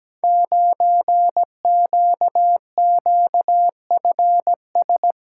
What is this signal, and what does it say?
Morse code, heard as a keyed tone, about 17 wpm, 700 Hz. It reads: 9QQFS
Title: Code Groups - 5 Letter:Numbers - 17wpm